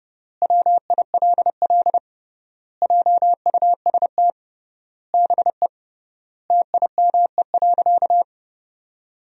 Morse code, heard as a keyed tone, about 30 wpm, 700 Hz. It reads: WILL JUST BE TIME.
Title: 3x Word-Spacing - Sentences from Top 200 Words - 30wpm